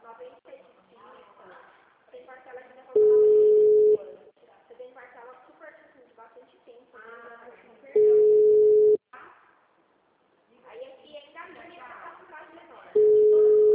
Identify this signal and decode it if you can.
Ringback tone (Europe, 425 Hz), with speech.